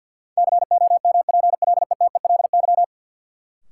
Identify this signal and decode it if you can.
Morse code, heard as a keyed tone, about 53 wpm, 700 Hz. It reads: COMPLETELY
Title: Sentences from Top 1100 Words - New Word Drill - 50wpm